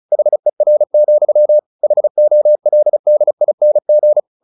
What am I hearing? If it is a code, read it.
Morse code, heard as a keyed tone, about 34 wpm, 600 Hz. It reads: HER, HOLDING
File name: the old man and the sea / the old man and the sea 1 35wpm